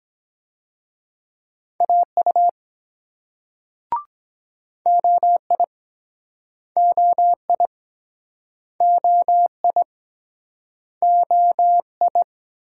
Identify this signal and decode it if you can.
Morse code, heard as a keyed tone, about 23 wpm, 700 Hz. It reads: AU OI OI OI OI